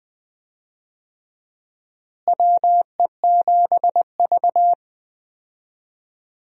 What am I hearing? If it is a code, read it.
Morse code, heard as a keyed tone, about 20 wpm, 700 Hz. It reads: WE7V
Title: USA Call Signs - 4 Characters - Repeated 2x - 20wpm